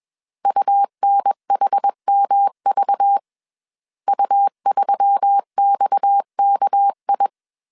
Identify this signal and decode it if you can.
Morse code, heard as a keyed tone, about 21 wpm, 800 Hz. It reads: UNHMV U3XKI
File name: CW / DET / 1 6 1500C S A 20wpm 2048